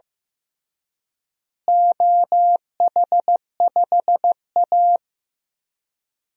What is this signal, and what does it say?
Morse code, heard as a keyed tone, about 15 wpm, 700 Hz. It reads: OH5A
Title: DXCC-100 Call Signs - 4 Characters - Repeated 3x - 15wpm